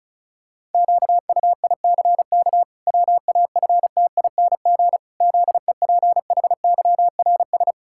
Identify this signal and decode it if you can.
Morse code, heard as a keyed tone, about 34 wpm, 700 Hz. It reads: QUICK WAFTING ZEPHYRS
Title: Pangram Challenge v2 - 35wpm